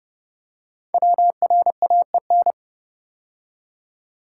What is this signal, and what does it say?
Morse code, heard as a keyed tone, about 29 wpm, 700 Hz. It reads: WRAEN
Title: Code Groups - 5 Letters - 30wpm